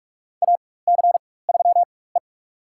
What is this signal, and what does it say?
Morse code, heard as a keyed tone, about 46 wpm, spaced as for 21 wpm, 700 Hz. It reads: AC3E
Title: USA Call Signs - 4 Characters - 4x Character Spacing - 45wpm